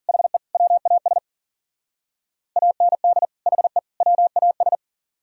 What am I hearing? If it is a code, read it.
Morse code, heard as a keyed tone, about 52 wpm, 700 Hz. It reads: HE WAS AND HE WAS